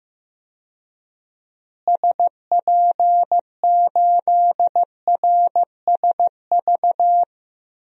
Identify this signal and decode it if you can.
Morse code, heard as a keyed tone, about 16 wpm, 700 Hz. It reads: SP8RSV
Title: DXCC-100 Call Signs - 6 Characters - Repeated 2x - 15wpm